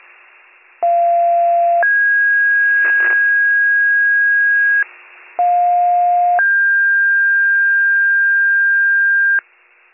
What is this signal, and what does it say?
Two-tone page, heard as a one-second tone, then a three-second tone, about 690 then 1650 Hz, about 710 then 1600 Hz. It is heard from 0.8 to 9.4 s.